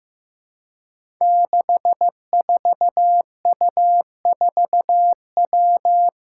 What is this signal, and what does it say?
Morse code, heard as a keyed tone, about 15 wpm, 700 Hz. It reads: 64U4W
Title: Five Challenging Letter-Number Combinations 15wpm